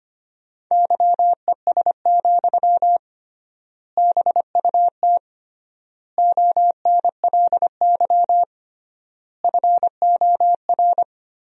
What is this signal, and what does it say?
Morse code, heard as a keyed tone, about 25 wpm, 700 Hz. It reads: YES, BUT ONLY FOR
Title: 3x Word-Spacing - Sentences from Top 500 Words - 25wpm